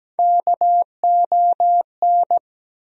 Morse code, heard as a keyed tone, about 17 wpm, 700 Hz. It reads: KON
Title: USA Call Sign Postfixes - 3 Letters 17wpm